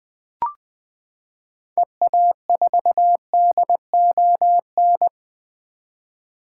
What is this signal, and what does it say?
Morse code, heard as a keyed tone, about 20 wpm, 700 Hz. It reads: EA4DON
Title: 2023 SOTA Activator Call Signs - 20wpm